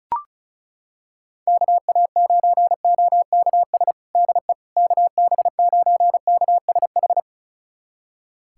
Morse code, heard as a keyed tone, about 34 wpm, 700 Hz. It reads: KA9OKS DE KB9KSH
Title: USA Callsign DE Callsign - 6 Characters - 35wpm